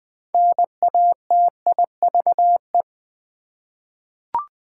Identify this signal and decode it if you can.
Morse code, heard as a keyed tone, about 20 wpm, 700 Hz. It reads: NATIVE
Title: Six Letter Words - 20wpm